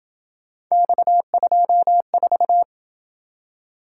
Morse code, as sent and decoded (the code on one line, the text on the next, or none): -..- ..--- ....-
X24